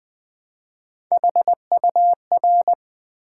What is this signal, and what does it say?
Morse code, heard as a keyed tone, about 20 wpm, 700 Hz. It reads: HUR
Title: Build A Word - 1 Letter At A Time - 5-Letter Words - 20wpm